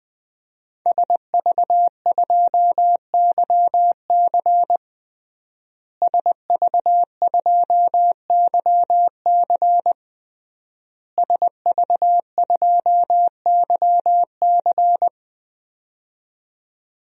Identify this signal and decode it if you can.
Morse code, heard as a keyed tone, about 21 wpm, 700 Hz. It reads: SV2YC SV2YC SV2YC
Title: DXCC-100 Call Signs - 5 Characters - Repeated 3x - 20wpm